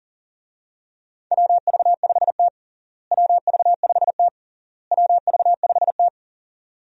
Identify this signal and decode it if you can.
Morse code, heard as a keyed tone, about 40 wpm, 700 Hz. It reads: WV5T WV5T WV5T
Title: USA Call Signs - 4 Characters - Repeated 3x - 40wpm